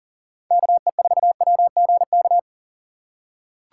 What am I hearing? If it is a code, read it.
Morse code, heard as a keyed tone, about 37 wpm, 700 Hz. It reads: KE4WGK
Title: USA Call Signs - 6 Characters - 2x Character Spacing - 40wpm